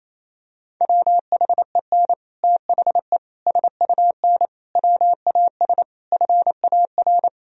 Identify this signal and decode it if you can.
Morse code, heard as a keyed tone, about 28 wpm, 700 Hz. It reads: WHEN THE SUN WAS FAR
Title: Sentences from Top 1000 Words - Review Part 3 - 28wpm